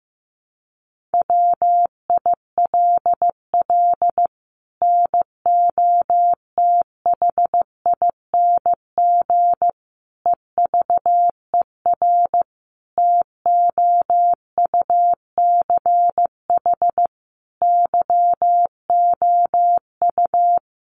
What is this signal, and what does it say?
Morse code, heard as a keyed tone, about 14 wpm, 700 Hz. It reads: WILL NOTHING EVER TOUCH YOU
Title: Sentences from Top 1200 Words 15wpm